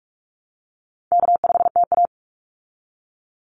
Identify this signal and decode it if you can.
Morse code, heard as a keyed tone, about 44 wpm, 700 Hz. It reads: K5TA
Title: Call Signs Repeated 2x - 45wpm